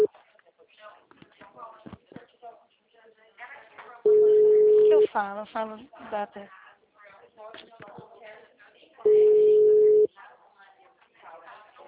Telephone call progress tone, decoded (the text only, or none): ringback tone